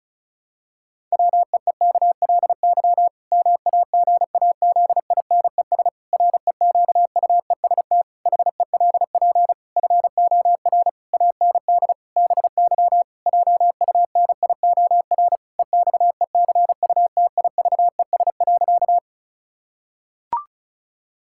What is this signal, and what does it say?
Morse code, heard as a keyed tone, about 36 wpm, 700 Hz. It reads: WEEKLY MAGAZINES REQUEST HELP FOR AND BY JUNIOR EXECUTIVES.
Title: Pangram Challenge v2 - 35wpm